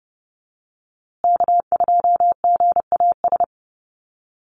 Morse code, heard as a keyed tone, about 30 wpm, 700 Hz. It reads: K2GAS
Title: USA Call Signs Repeated 2x 30wpm